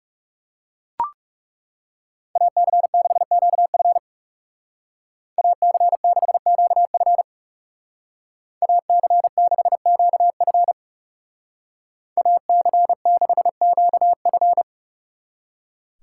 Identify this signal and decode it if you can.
Morse code, heard as a keyed tone, about 39 wpm, 700 Hz. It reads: AC6QF AC6QF AC6QF AC6QF